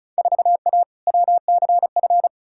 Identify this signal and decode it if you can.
Morse code, heard as a keyed tone, about 34 wpm, 700 Hz. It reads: 4A WCF
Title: Field Day Exchanges - 35wpm